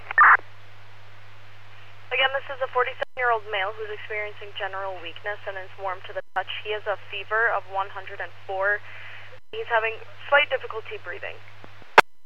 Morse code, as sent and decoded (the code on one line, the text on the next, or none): none